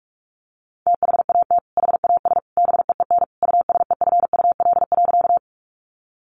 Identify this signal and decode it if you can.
Morse code, heard as a keyed tone, about 44 wpm, 700 Hz. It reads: THAT HAS BEEN USEFUL.
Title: Sentences from Top 700 Words 45wpm